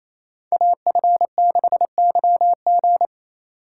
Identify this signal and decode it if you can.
Morse code, heard as a keyed tone, about 28 wpm, 700 Hz. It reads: AF6YG
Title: USA Call Signs - 4x Character Spacing - 28wpm